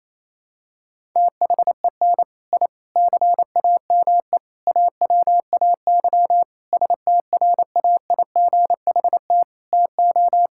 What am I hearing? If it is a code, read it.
Morse code, heard as a keyed tone, about 28 wpm, 700 Hz. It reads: THEN I CAME AWAY STRAIGHT TO
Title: Sentences from Top 1000 Words - Review Part 6 - 28wpm